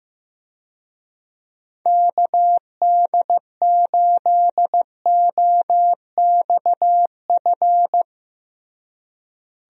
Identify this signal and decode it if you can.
Morse code, heard as a keyed tone, about 16 wpm, 700 Hz. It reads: KD8OXF
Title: USA Call Signs - 6 Characters - Repeated 3x - 15wpm